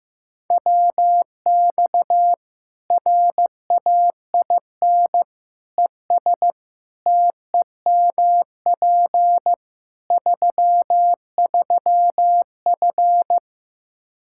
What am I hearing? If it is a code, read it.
Morse code, heard as a keyed tone, about 15 wpm, 700 Hz. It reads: WX RAIN ES TEMP 33F
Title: Practice QSO - Exchange 2 - Info - 15wpm